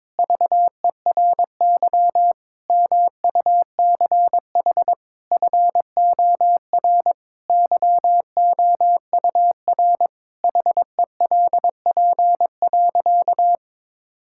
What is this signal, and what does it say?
Morse code, heard as a keyed tone, about 22 wpm, 700 Hz. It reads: VERY MUCH FOR YOUR HELP.
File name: practice / 22wpm / Sentences with 4-Word Phrases - 22wpm